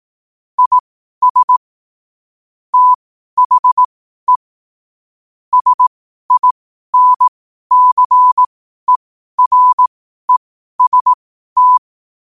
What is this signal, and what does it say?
Morse code, heard as a keyed tone, about 18 wpm, spaced as for 12 wpm, 1000 Hz. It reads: IS THE SINCEREST
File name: qotdfiles / 12wpm